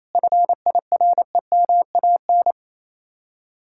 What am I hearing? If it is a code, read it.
Morse code, heard as a keyed tone, about 37 wpm, 700 Hz. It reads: FIREMAN